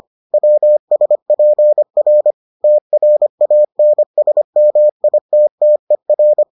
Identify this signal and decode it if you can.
Morse code, heard as a keyed tone, about 25 wpm, 600 Hz. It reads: WSPR TRANSMITTER